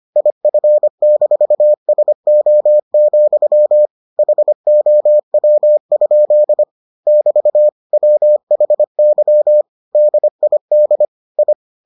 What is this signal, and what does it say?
Morse code, heard as a keyed tone, about 25 wpm, 600 Hz. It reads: IF-SO, HOW? <BT> WHY DID I